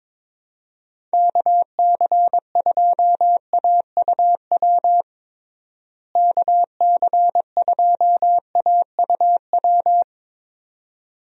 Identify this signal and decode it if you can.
Morse code, heard as a keyed tone, about 22 wpm, 700 Hz. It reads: KC2AUW KC2AUW
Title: USA Call Signs - 6 Characters - Repeated 2x - 22wpm